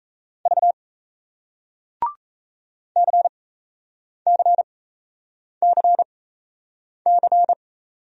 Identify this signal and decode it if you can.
Morse code, heard as a keyed tone, about 37 wpm, spaced as for 8 wpm, 700 Hz. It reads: V CCCC